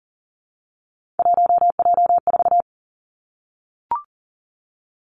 Morse code, heard as a keyed tone, about 40 wpm, 700 Hz. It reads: 1J4
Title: Three Challenging Letter-Number Combinations 40wpm